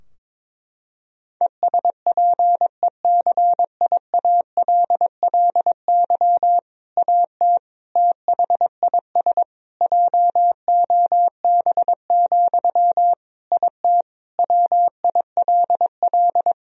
Morse code, heard as a keyed tone, about 22 wpm, 700 Hz. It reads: ESPECIALLY AT THIS JOB, IT WILL
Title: Sentences from Top 1000 Words - Review Part 4 - 22wpm